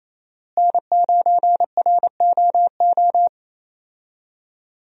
Morse code, heard as a keyed tone, about 27 wpm, 700 Hz. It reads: N9ROO
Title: USA Call Signs - 5 Characters - Repeated 3x - 28wpm